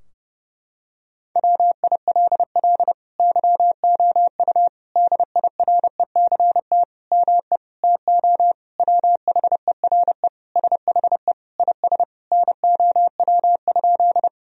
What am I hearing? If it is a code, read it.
Morse code, heard as a keyed tone, about 30 wpm, 700 Hz. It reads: WILL YOU DIRECT ME TO WHERE SHE IS NOW?
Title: Sentences from Top 1000 Words - Review Part 2 - 30wpm